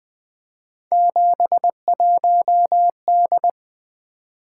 Morse code, heard as a keyed tone, about 20 wpm, 700 Hz. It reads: 71D
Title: Three Letter-Number Combinations - 20wpm